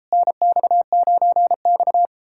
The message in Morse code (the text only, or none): NX9X